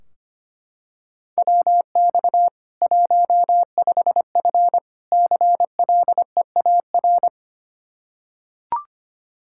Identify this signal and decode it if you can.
Morse code, heard as a keyed tone, about 25 wpm, 700 Hz. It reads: WX 15F CLEAR
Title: Practice QSO - Exchange 2 - Info - 25wpm